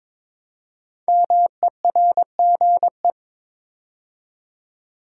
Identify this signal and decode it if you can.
Morse code, heard as a keyed tone, about 22 wpm, 700 Hz. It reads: MERGE
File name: 5 Letter Words - First 3 Letters Given - 22wpm